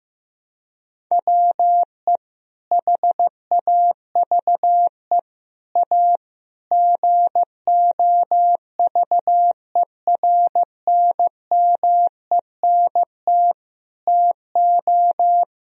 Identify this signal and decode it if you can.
Morse code, heard as a keyed tone, about 15 wpm, 700 Hz. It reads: WE HAVE A GOVERNMENT TO
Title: Sentences from Top 1000 Words - Review Part 1 - 15wpm